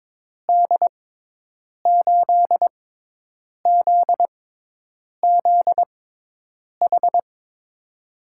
Morse code, heard as a keyed tone, about 22 wpm, spaced as for 8 wpm, 700 Hz. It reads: D8ZZH
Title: USA Call Signs - 6 Characters - 6x Character Spacing - 22wpm